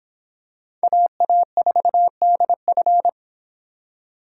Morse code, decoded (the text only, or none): AA4DF